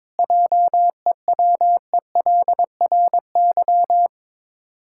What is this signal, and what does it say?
Morse code, heard as a keyed tone, about 22 wpm, 690 Hz. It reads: JEWELRY
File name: Difficult Words with Hints - 22wpm